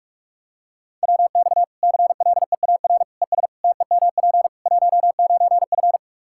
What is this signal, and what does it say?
Morse code, heard as a keyed tone, about 46 wpm, 700 Hz. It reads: WX CLEAR ES TEMP 19F
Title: Practice QSO - Exchange 2 - Info - 45wpm